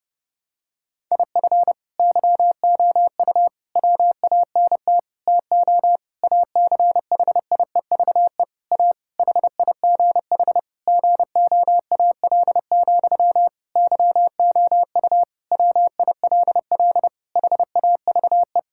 Morse code, heard as a keyed tone, about 32 wpm, 700 Hz. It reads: IF YOU WANT TO ACHIEVE A HIGH GOAL, YOU WILL HAVE
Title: Sentences from Top 1000 Words - Review Part 6 - 30wpm